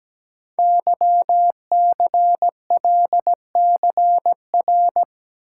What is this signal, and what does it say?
Morse code, heard as a keyed tone, about 17 wpm, 700 Hz. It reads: YCLCR
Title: Code Groups - 5 Letters - 17wpm